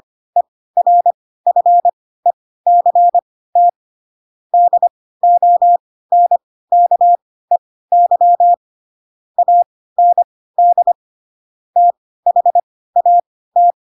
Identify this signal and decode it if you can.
Morse code, heard as a keyed tone, about 25 wpm, spaced as for 16 wpm, 700 Hz. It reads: ERFECT DONKEY AND THAT
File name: Pinocchio / Pinocchio ch4 10 25 ik1hge